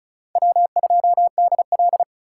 Morse code, heard as a keyed tone, about 44 wpm, 700 Hz. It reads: W2DL